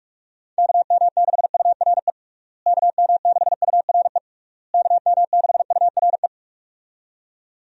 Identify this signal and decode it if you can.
Morse code, heard as a keyed tone, about 46 wpm, 700 Hz. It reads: KM6URE KM6URE KM6URE
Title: USA Call Signs - 6 Characters - Repeated 3x - 45wpm